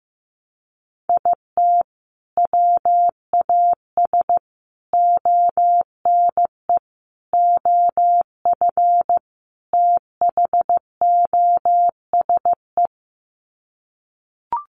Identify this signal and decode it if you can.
Morse code, heard as a keyed tone, about 15 wpm, 700 Hz. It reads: IT WAS ONE OF THOSE
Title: Sets of 5 Words Encore 15wpm